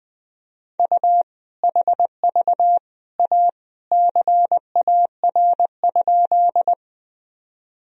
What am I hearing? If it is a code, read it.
Morse code, heard as a keyed tone, about 20 wpm, 700 Hz. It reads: U HV A CAR?
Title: Sentences from Top 300 Words and Abbreviations 20wpm